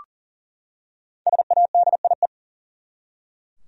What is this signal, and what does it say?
Morse code, heard as a keyed tone, about 40 wpm, 700 Hz. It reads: SADIE
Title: USA Common Names - Max 5 Letters - 40wpm